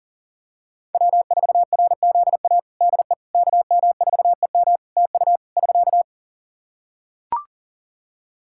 Morse code, heard as a keyed tone, about 40 wpm, 700 Hz. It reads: W4RZA DE KM4EM TU <SK>
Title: Practice QSO - Exchange 4 - Ending - 40wpm